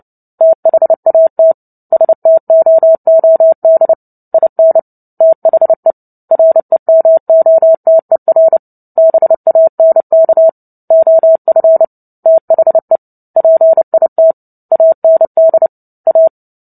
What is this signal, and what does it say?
Morse code, heard as a keyed tone, about 30 wpm, 650 Hz. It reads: THAT STOOD IN THE REMOTER BANK OF THE PIT AND A
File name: Book Two / 03-Chapter 3 - 29 WPM